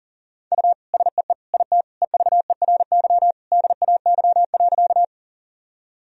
Morse code, heard as a keyed tone, about 40 wpm, 700 Hz. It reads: U SEE IT EVERY DAY.
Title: Sentences from Top 200 Words and Abbreviations 40wpm